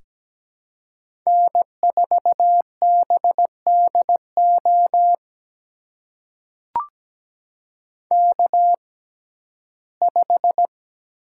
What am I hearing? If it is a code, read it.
Morse code, heard as a keyed tone, about 17 wpm, spaced as for 12 wpm, 700 Hz. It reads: N4BDO K5
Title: USA Call Signs - 5 Characters - 6x Character Spacing - 17wpm